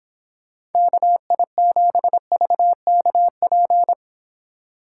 Morse code, heard as a keyed tone, about 31 wpm, 700 Hz. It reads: KI7VKP